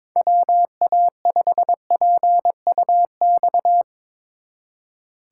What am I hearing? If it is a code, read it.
Morse code, heard as a keyed tone, about 24 wpm, 700 Hz. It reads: WA5PUX
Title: Speed Racing - USA 2x3 Call Signs 22wpm